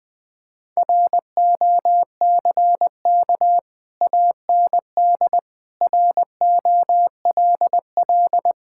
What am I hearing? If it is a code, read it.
Morse code, heard as a keyed tone, about 20 wpm, 700 Hz. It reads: ROCK AND ROLL
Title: Sets of 6 Words - 20wpm